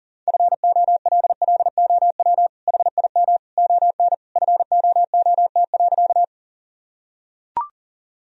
Morse code, heard as a keyed tone, about 37 wpm, 700 Hz. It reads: FOLLOW HIM ON FOOT.
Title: Sentences from Top 1000 Words - Review Part 3 - 40wpm